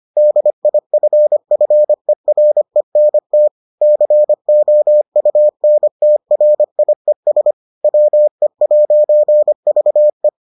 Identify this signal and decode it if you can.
Morse code, heard as a keyed tone, about 25 wpm, 600 Hz. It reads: DIFFERENT COUNTRIES WE'VE